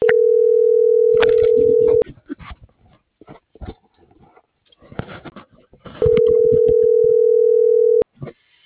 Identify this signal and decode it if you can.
Ringback tone (North America), 440 + 480 Hz.